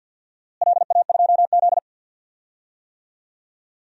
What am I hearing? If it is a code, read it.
Morse code, heard as a keyed tone, about 50 wpm, 700 Hz. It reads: LA2Z
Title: DXCC-100 Call Signs - 4 Characters - Repeated 3x - 50wpm